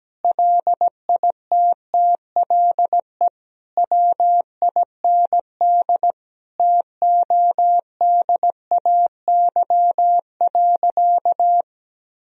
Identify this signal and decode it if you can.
Morse code, heard as a keyed tone, about 17 wpm, 700 Hz. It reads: LITTLE WIND TODAY.
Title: Sentences from Top 1000 Words - Review Part 2 - 17wpm